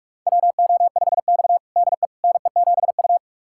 Morse code, heard as a keyed tone, about 46 wpm, 700 Hz. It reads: WO5X DE NE7U
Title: USA Callsign DE Callsign - 4 Characters - 45wpm